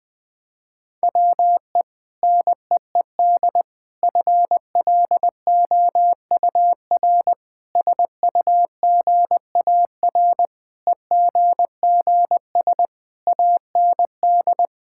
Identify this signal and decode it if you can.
Morse code, heard as a keyed tone, about 21 wpm, 700 Hz. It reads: WE NEED FLOUR SUGAR EGGS AND
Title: Sentences with 6-Word Phrases - 20wpm